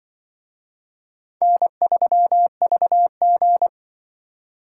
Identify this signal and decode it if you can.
Morse code, heard as a keyed tone, about 25 wpm, 700 Hz. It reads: N3VG